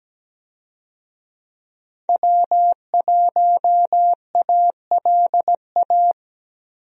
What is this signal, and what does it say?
Morse code, heard as a keyed tone, about 17 wpm, 700 Hz. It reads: W1ALA
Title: USA Call Signs - 5 Characters - Repeated 3x - 17wpm